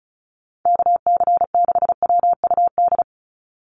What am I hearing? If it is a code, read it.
Morse code, heard as a keyed tone, about 34 wpm, 700 Hz. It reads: KC6WUD
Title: USA Call Signs 35wpm